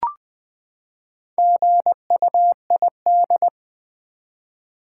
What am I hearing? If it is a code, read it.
Morse code, heard as a keyed tone, about 20 wpm, 700 Hz. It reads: GUID
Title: Build A Word - 4 Letters At A Time - 8-Letter Words - 20wpm